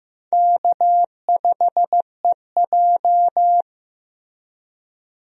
Morse code, heard as a keyed tone, about 15 wpm, 700 Hz. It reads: K5EJ
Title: USA Call Signs - 4 Characters - Repeated 3x - 15wpm